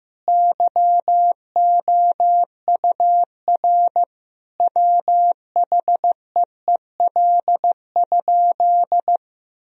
Morse code, heard as a keyed tone, about 15 wpm, 700 Hz. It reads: YOUR WHEEL?
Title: Sentences from Top 1000 Words - Review Part 3 - 15wpm